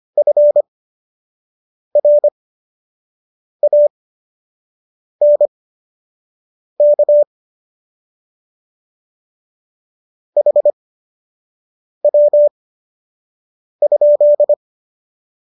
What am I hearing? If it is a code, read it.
Morse code, heard as a keyed tone, about 25 wpm, spaced as for 6 wpm, 590 Hz. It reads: FRANK HW?